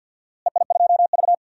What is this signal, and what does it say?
Morse code, heard as a keyed tone, about 51 wpm, 700 Hz. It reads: EI2V
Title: DXCC-100 Call Signs - 4 Characters - Repeated 2x - 50wpm